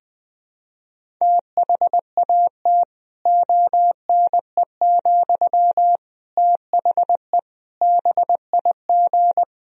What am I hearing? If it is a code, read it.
Morse code, heard as a keyed tone, about 20 wpm, 700 Hz. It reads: THAT ONE, THE BIG
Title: Sentences from Top 1000 Words - Review Part 1 - 20wpm